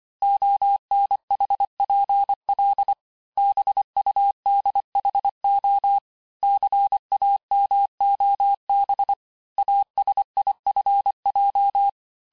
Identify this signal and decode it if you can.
Morse code, heard as a keyed tone, about 25 wpm, 800 Hz. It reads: ONHPL BUDHO CAMOB ASIFJ
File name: texte-chiffre-vitesse-23mpm